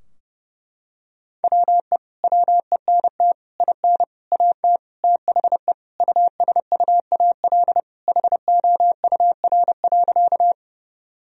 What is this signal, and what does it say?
Morse code, heard as a keyed tone, about 30 wpm, 700 Hz. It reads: WE WENT IN AT THE USUAL HOUR.
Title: Sentences from Top 1000 Words - Review Part 2 - 30wpm